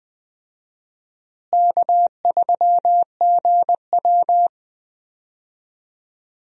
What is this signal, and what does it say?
Morse code, heard as a keyed tone, about 20 wpm, 700 Hz. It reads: K3GW
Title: USA Call Signs - 4 Characters - Repeated 3x - 20wpm